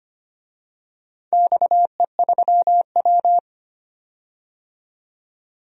Morse code, heard as a keyed tone, about 25 wpm, 700 Hz. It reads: XE3W
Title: DXCC-100 Call Signs - 4 Characters - 25wpm